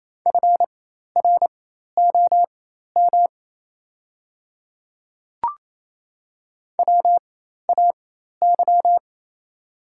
Morse code, heard as a keyed tone, about 27 wpm, spaced as for 13 wpm, 700 Hz. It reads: FROM WAY
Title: 4x Character Spacing - Top 100 Words - 28wpm